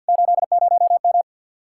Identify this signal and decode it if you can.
Morse code, heard as a keyed tone, about 60 wpm, 700 Hz. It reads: 80M
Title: Speed-Racing Common QSO Phrases - 50wpm